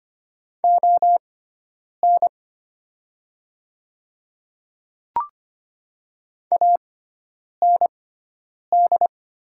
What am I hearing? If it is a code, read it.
Morse code, heard as a keyed tone, about 25 wpm, spaced as for 9 wpm, 700 Hz. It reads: ON AND
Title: 6x Character Spacing - Top 100 Words - 25wpm